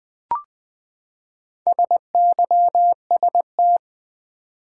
Morse code, heard as a keyed tone, about 20 wpm, 700 Hz. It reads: SYST
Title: Build A Word - 4 Letters At A Time - 10-Letter Words - 20wpm